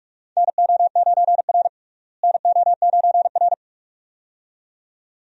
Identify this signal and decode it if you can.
Morse code, heard as a keyed tone, about 42 wpm, 700 Hz. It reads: NO9R NO9R
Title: USA Call Signs - 4 Characters - Repeated 2x - 45wpm